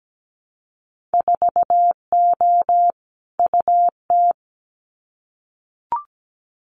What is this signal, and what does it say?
Morse code, heard as a keyed tone, about 17 wpm, 700 Hz. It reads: 4O UT
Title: Winter Field Day Exchanges Encore 17wpm